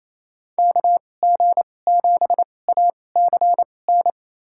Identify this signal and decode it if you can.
Morse code, heard as a keyed tone, about 28 wpm, spaced as for 20 wpm, 700 Hz. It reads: KG7ACN
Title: USA Call Signs - 6 Characters - 2x Character Spacing - 28wpm